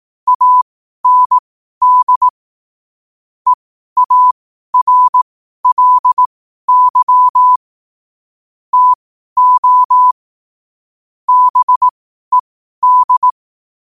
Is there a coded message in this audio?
Yes: Morse code from 0.3 to 13.3 s, heard as a keyed tone, about 18 wpm, spaced as for 12 wpm, 1000 Hz. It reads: AND EARLY TO BED